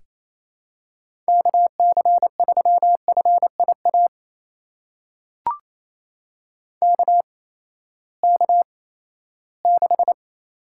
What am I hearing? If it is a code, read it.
Morse code, heard as a keyed tone, about 28 wpm, spaced as for 16 wpm, 700 Hz. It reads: KC3FIA KK6
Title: USA Call Signs - 6 Characters - 8x Character Spacing - 28wpm